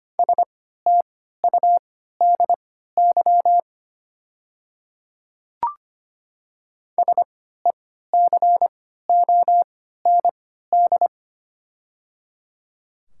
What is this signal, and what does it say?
Morse code, heard as a keyed tone, about 25 wpm, spaced as for 14 wpm, 700 Hz. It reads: STUDY SECOND
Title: 3x Character Spacing - Top 500 Words - 25wpm